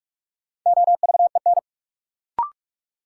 Morse code, heard as a keyed tone, about 46 wpm, 700 Hz. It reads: OVEN
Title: Sentences from Top 1200 Words - New Word Drill Alt - 45wpm